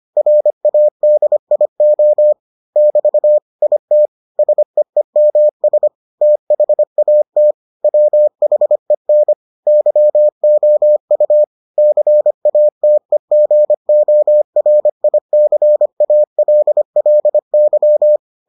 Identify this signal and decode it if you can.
Morse code, heard as a keyed tone, about 25 wpm, 600 Hz. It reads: RADIO <BT> IT SEEMS THAT WHEN YOU CATEGORICALLY